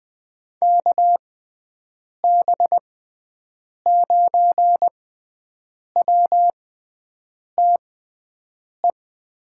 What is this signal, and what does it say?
Morse code, heard as a keyed tone, about 21 wpm, spaced as for 7 wpm, 700 Hz. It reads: KB9WTE
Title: USA Call Signs - 6 Characters - 6x Character Spacing - 20wpm